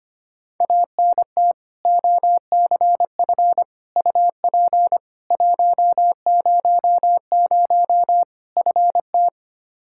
Morse code, heard as a keyed tone, about 25 wpm, 700 Hz. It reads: ANT OCF UP 100 FT
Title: K1USN SST QSO Practice 25wpm